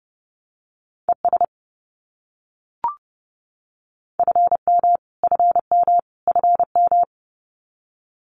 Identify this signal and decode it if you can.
Morse code, heard as a keyed tone, about 30 wpm, 700 Hz. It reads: ES FM FM FM
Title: Abbreviations Repeated 3x - 30wpm